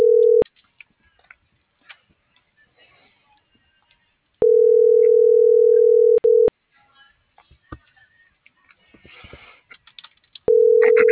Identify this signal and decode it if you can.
Ringback tone (North America), 440 + 480 Hz.